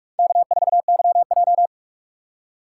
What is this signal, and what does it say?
Morse code, heard as a keyed tone, about 42 wpm, 700 Hz. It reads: K4YJ
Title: USA Call Signs - 4 Characters - 3x Character Spacing - 45wpm